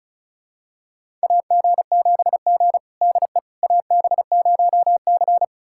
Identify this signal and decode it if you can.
Morse code, heard as a keyed tone, about 36 wpm, 700 Hz. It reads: AG7G DE AB0C
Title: USA Callsign DE Callsign - 4 Characters - 35wpm